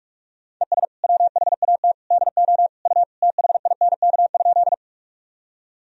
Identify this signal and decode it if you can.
Morse code, heard as a keyed tone, about 46 wpm, 700 Hz. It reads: ES WHAT DO U THINK?
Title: Sentences from Top 200 Words and Abbreviations 45wpm